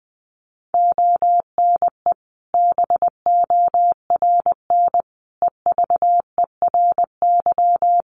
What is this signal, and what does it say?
Morse code, heard as a keyed tone, about 20 wpm, 700 Hz. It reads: ONE BORN EVERY
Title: Sentences from Top 1400 Words 20wpm